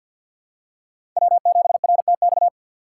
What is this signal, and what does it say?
Morse code, heard as a keyed tone, about 53 wpm, 700 Hz. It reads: W7RTX
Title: USA Call Signs - 3x Character Spacing - 50wpm